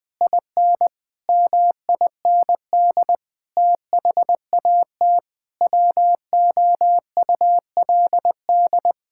Morse code, heard as a keyed tone, about 21 wpm, 700 Hz. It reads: IN MIND THAT WOULD
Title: Sentences with 9-Word Phrases - 20wpm